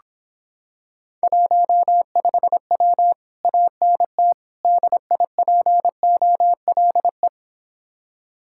Morse code, heard as a keyed tone, about 26 wpm, 700 Hz. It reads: 15W ANT DIPOLE